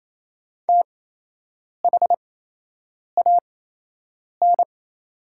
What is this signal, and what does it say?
Morse code, heard as a keyed tone, about 28 wpm, spaced as for 8 wpm, 700 Hz. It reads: THAN
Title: 8x Character Spacing - Top 100 Words - 28wpm